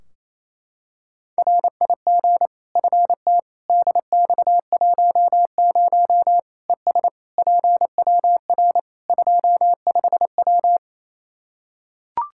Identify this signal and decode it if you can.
Morse code, heard as a keyed tone, about 28 wpm, 700 Hz. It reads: RIG FT DX10 ES PWR 25W
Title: Practice QSO - Exchange 2 - Info - 28wpm